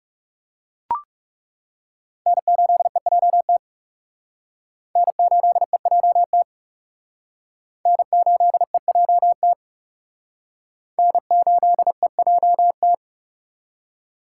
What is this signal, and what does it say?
Morse code, heard as a keyed tone, about 40 wpm, 700 Hz. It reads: N8EJT N8EJT N8EJT N8EJT